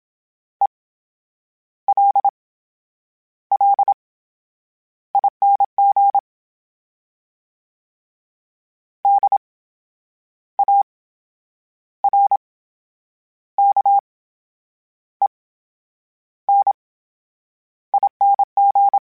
Morse code, heard as a keyed tone, about 27 wpm, spaced as for 9 wpm, 800 Hz. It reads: ELLING DARKENING
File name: Special INGs / Special INGs-mp3s / VerySlowToFaster Ing @25